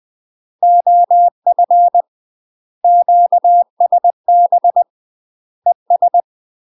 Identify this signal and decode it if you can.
Morse code, heard as a keyed tone, about 20 wpm, 700 Hz. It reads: OF QSB ES
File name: UK MASTER QSOs TRAINING / 20WPM / UK MASTER QSOs 0001